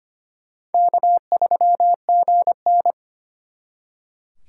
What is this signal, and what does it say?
Morse code, heard as a keyed tone, about 32 wpm, 700 Hz. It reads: K3GN